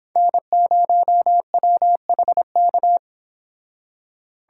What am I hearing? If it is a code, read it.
Morse code, heard as a keyed tone, about 30 wpm, 700 Hz. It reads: N0WHK